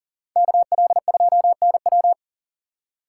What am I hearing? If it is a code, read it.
Morse code, heard as a keyed tone, about 52 wpm, 700 Hz. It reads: KL2NW